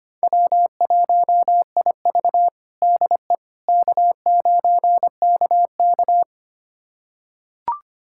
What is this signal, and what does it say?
Morse code, heard as a keyed tone, about 25 wpm, 700 Hz. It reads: W1IV DE K9KK
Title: USA Callsign DE Callsign - 4 Characters - 25wpm